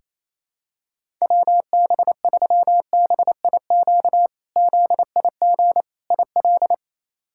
Morse code, heard as a keyed tone, about 28 wpm, 700 Hz. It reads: WB3BIQ ZIG IL
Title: K1USN SST QSO Practice - 28wpm